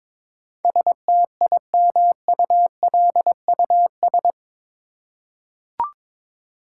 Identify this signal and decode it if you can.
Morse code, heard as a keyed tone, about 22 wpm, 700 Hz. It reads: STIMULUS
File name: practice / 22wpm / Eight Letter Words - 22wpm